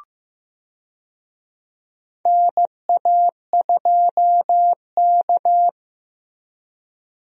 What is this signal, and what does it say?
Morse code, heard as a keyed tone, about 15 wpm, 700 Hz. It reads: NA2K
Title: USA Call Signs - 4 Characters - Repeated 2x - 15wpm